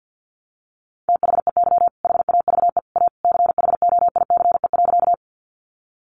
Morse code, heard as a keyed tone, about 50 wpm, 700 Hz. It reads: THEY HAVE A CHOICE.
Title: Sentences from Top 900 Words 50wpm